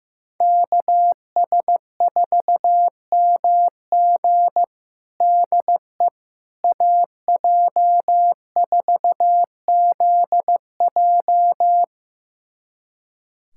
Morse code, heard as a keyed tone, about 15 wpm, 700 Hz. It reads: KS4MG DE AJ4ZJ
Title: USA Callsign DE Callsign - 5 Characters - 15wpm